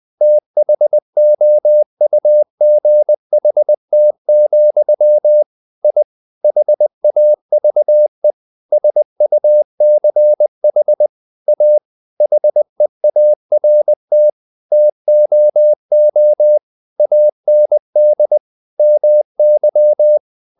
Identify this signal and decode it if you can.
Morse code, heard as a keyed tone, about 20 wpm, 600 Hz. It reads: THOUGHT, I HAVE SUCH A HEART TOO AND MY
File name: the old man and the sea / the old man and the sea part 1 20wpm